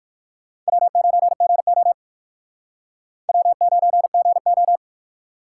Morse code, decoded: W9GO W9GO

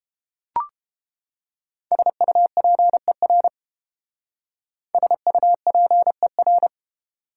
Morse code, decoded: SUPER SUPER

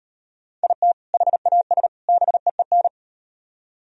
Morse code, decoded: IT HAS BEEN

